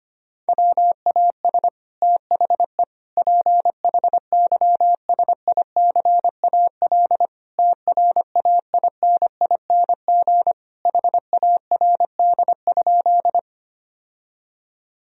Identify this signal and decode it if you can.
Morse code, heard as a keyed tone, about 25 wpm, 700 Hz. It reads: WAS THE PHYSICAL TRAINING HARD?
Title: Sentences from Top 1000 Words - Review Part 4 - 25wpm